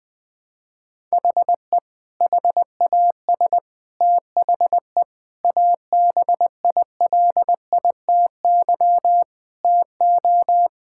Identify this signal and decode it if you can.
Morse code, heard as a keyed tone, about 20 wpm, 700 Hz. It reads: HE HAS THE ABILITY TO
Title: Sentences with 3-Word Phrases - 20wpm